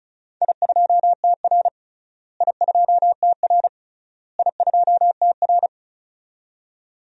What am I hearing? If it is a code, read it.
Morse code, heard as a keyed tone, about 36 wpm, 700 Hz. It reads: I2TR I2TR I2TR
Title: DXCC-100 Call Signs - 4 Characters - Repeated 3x - 35wpm